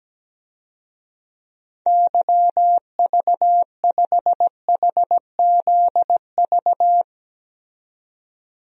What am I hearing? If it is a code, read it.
Morse code, heard as a keyed tone, about 17 wpm, 700 Hz. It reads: YV5HZV
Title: DXCC-100 Call Signs - 6 Characters - 17wpm